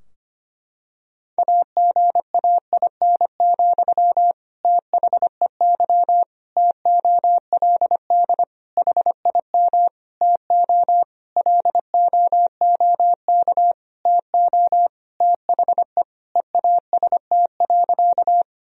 Morse code, heard as a keyed tone, about 25 wpm, 700 Hz. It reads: AGAIN, THEY TOLD HIM TO LOOK TO THE EAST.
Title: Sentences from Top 1000 Words - Review Part 3 - 25wpm